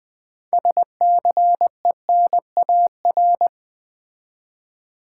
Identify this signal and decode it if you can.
Morse code, heard as a keyed tone, about 20 wpm, 700 Hz. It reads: SCENAR
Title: Build A Word - 3 Letters At A Time - 8-Letter Words - 20wpm